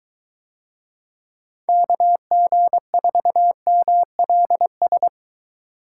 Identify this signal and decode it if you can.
Morse code, heard as a keyed tone, about 22 wpm, 700 Hz. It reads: KG4MLS